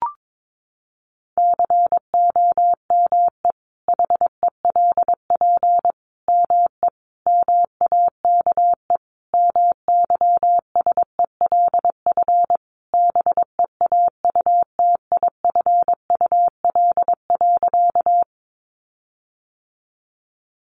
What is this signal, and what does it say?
Morse code, heard as a keyed tone, about 22 wpm, 700 Hz. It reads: COME HELP ME MAKE MYSELF BEAUTIFUL.
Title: Sentences from Top 1300 Words 22wpm